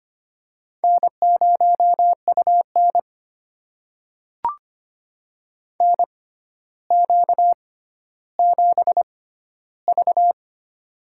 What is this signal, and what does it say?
Morse code, heard as a keyed tone, about 25 wpm, spaced as for 13 wpm, 700 Hz. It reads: N0UN NQ7V
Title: USA Call Signs - 4 Characters - 6x Character Spacing - 25wpm